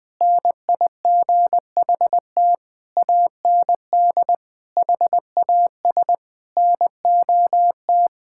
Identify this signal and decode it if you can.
Morse code, heard as a keyed tone, about 21 wpm, 700 Hz. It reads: NIGHT AND HAS NOT
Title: Sentences from Top 1600 Words 20wpm